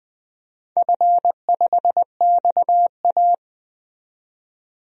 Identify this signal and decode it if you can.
Morse code, heard as a keyed tone, about 20 wpm, 700 Hz. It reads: F5XA
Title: DXCC-100 Call Signs - 4 Characters - 20wpm